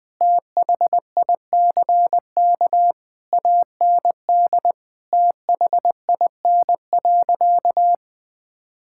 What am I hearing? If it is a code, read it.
Morse code, heard as a keyed tone, about 20 wpm, 700 Hz. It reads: THICK AND THIN.
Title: Sentences from Top 1600 Words 20wpm